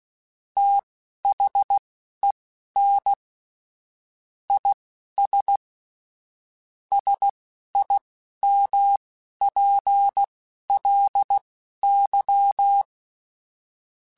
Morse code, heard as a keyed tone, about 16 wpm, spaced as for 11 wpm, 800 Hz. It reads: THEN IS SIMPLY
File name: Technical Subjects / Technical Subjects-mp3s / XC @10